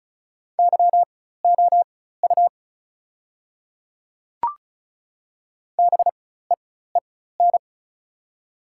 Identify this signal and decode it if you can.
Morse code, heard as a keyed tone, about 36 wpm, spaced as for 16 wpm, 700 Hz. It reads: YOU BEEN
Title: 4x Character Spacing - Top 200 Words - 35wpm